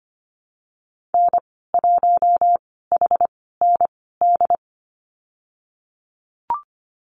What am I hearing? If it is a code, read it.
Morse code, heard as a keyed tone, about 25 wpm, spaced as for 16 wpm, 700 Hz. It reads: N1HND
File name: Lesson 15g - Number 5 - Call Signs 25wpm